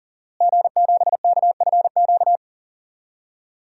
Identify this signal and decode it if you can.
Morse code, heard as a keyed tone, about 40 wpm, 700 Hz. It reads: G7KFQ
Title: DXCC-100 Call Signs - 5 Characters - Repeated 2x - 40wpm